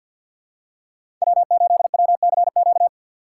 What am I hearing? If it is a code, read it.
Morse code, heard as a keyed tone, about 52 wpm, 700 Hz. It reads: W8WCQ
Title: USA Call Signs - 5 Characters - 4x Character Spacing - 50wpm